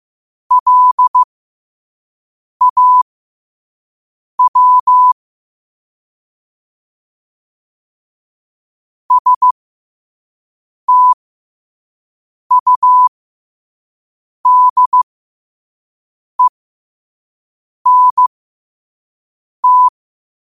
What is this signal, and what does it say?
Morse code, heard as a keyed tone, about 15 wpm, spaced as for 5 wpm, 1000 Hz. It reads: LAW STUDENT
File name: qotdfiles / 5wpm